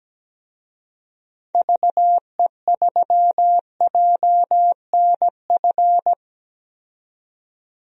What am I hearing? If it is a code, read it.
Morse code, heard as a keyed tone, about 17 wpm, 700 Hz. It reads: VE3JNF
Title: DXCC-100 Call Signs - 6 Characters - 17wpm